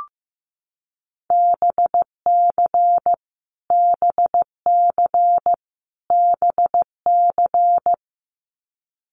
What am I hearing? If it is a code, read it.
Morse code, heard as a keyed tone, about 15 wpm, 700 Hz. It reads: BC BC BC
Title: Canadian Provinces and Territories - Repeated 3x - 15wpm